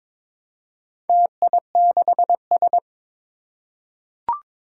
Morse code, heard as a keyed tone, about 22 wpm, 700 Hz. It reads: TI6S
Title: Four Challenging Letter-Number Combinations 22wpm